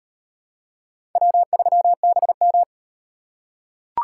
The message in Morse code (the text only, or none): W3BM